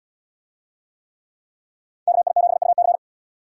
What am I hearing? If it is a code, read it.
Morse code, heard as a keyed tone, about 78 wpm, 700 Hz. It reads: KI7UC